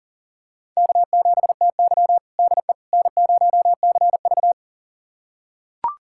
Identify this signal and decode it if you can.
Morse code, heard as a keyed tone, about 40 wpm, 700 Hz. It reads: K7TY DE N0CV
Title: USA Callsign DE Callsign - 4 Characters - 40wpm